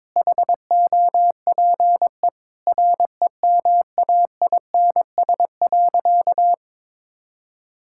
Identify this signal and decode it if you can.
Morse code, heard as a keyed tone, about 22 wpm, 700 Hz. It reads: HOPE REMAINS.
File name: practice / 22wpm / Sentences with 2-Word Phrases - 22wpm